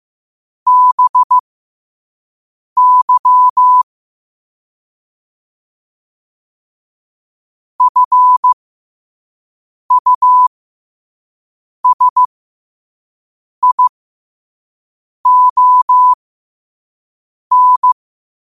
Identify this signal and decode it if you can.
Morse code, heard as a keyed tone, about 15 wpm, spaced as for 5 wpm, 1000 Hz. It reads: BY FUSION
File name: qotdfiles / 5wpm